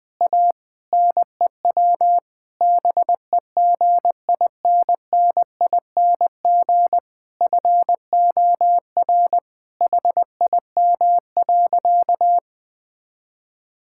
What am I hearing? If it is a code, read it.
Morse code, heard as a keyed tone, about 21 wpm, 700 Hz. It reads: A NEW BEGINNING FOR HIM.
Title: Sentences from Top 1000 Words - Review Part 5 - 20wpm